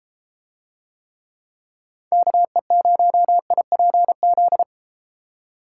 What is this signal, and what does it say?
Morse code, heard as a keyed tone, about 33 wpm, 700 Hz. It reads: KE0IPZ